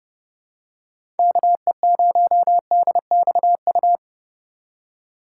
Morse code, heard as a keyed tone, about 30 wpm, 700 Hz. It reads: KE0DXU
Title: USA Call Signs - 6 Characters - 2x Character Spacing - 30wpm